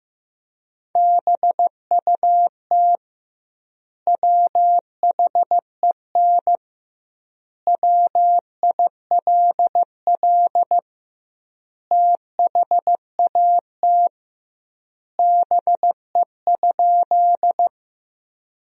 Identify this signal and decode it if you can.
Morse code, heard as a keyed tone, about 16 wpm, 700 Hz. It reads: BUT WHEN WILL THAT BE?
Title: 2x Word-Spacing - Sentences from Top 100 Words - 15wpm